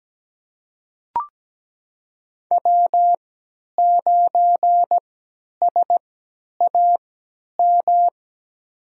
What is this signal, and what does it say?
Morse code, heard as a keyed tone, about 17 wpm, spaced as for 10 wpm, 700 Hz. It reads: W9SAM
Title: USA Call Signs - 5 Characters - 3x Character Spacing - 17wpm